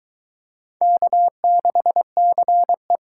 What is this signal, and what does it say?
Morse code, heard as a keyed tone, about 21 wpm, 700 Hz. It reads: K6CE